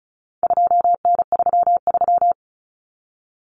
Morse code, heard as a keyed tone, about 34 wpm, 700 Hz. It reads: 2N33
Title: Common Cut Numbers - 4-Digit 35wpm